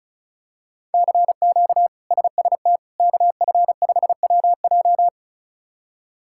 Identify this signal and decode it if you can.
Morse code, heard as a keyed tone, about 36 wpm, 700 Hz. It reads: CQ SST KF5WJ
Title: K1USN SST QSO Practice - 35wpm